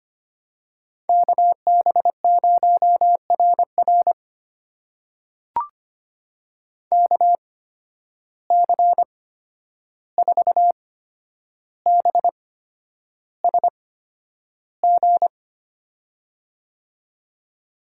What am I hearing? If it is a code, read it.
Morse code, heard as a keyed tone, about 25 wpm, spaced as for 10 wpm, 700 Hz. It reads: KB0RR KC4BSG
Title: USA Call Signs - 8x Character Spacing - 25wpm